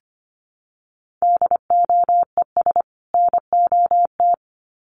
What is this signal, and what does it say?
Morse code, heard as a keyed tone, about 25 wpm, 700 Hz. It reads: DOES NOT
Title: Sets of 2 Words 25wpm